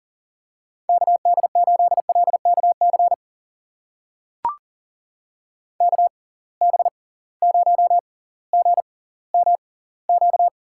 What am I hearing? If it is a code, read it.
Morse code, heard as a keyed tone, about 39 wpm, spaced as for 20 wpm, 700 Hz. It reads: KD8LKC KB0GMQ
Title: USA Call Signs - 6 Characters - 6x Character Spacing - 40wpm